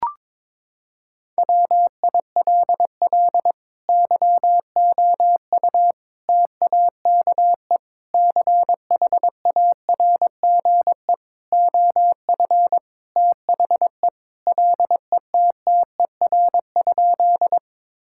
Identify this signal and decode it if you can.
Morse code, heard as a keyed tone, about 22 wpm, 700 Hz. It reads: WILL YOU TAKE CHARGE OF THE LETTER?
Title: Sentences from Top 1000 Words - Review Part 6 - 22wpm